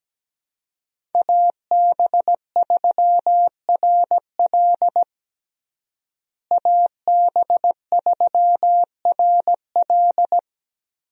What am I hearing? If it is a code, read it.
Morse code, heard as a keyed tone, about 17 wpm, 700 Hz. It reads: AB3RL AB3RL
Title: USA Call Signs - 5 Characters - Repeated 3x - 17wpm